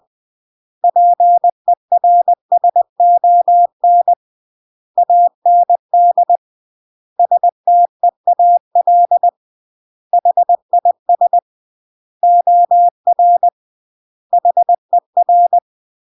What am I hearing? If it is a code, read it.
Morse code, heard as a keyed tone, about 20 wpm, 700 Hz. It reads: PERSON AND STEAL HIS OR HER
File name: CW / jodorowsky / 82 commandments 20-20